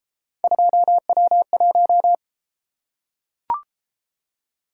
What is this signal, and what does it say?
Morse code, heard as a keyed tone, about 30 wpm, 700 Hz. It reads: 2W1